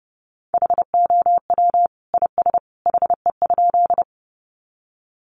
Morse code, heard as a keyed tone, about 30 wpm, 700 Hz. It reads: HOW IS HE?
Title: Sentences from Top 100 Words 30wpm